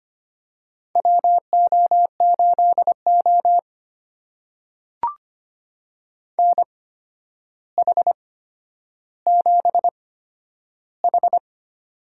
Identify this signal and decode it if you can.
Morse code, heard as a keyed tone, about 25 wpm, spaced as for 11 wpm, 700 Hz. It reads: WO8O NH7H
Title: USA Call Signs - 4 Characters - 8x Character Spacing - 25wpm